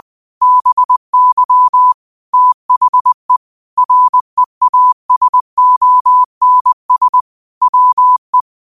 Morse code, heard as a keyed tone, about 20 wpm, 1000 Hz. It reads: BY THE REASONS WE
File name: qotdfiles / 20wpm